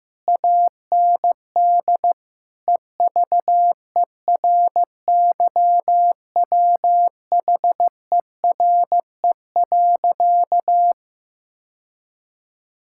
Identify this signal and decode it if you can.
Morse code, heard as a keyed tone, about 16 wpm, 700 Hz. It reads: AND EVERYWHERE.
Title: Sentences from Top 1600 Words 15wpm